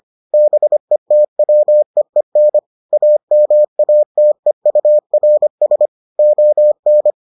Morse code, heard as a keyed tone, about 25 wpm, 600 Hz. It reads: BETWEEN AMATEURS ON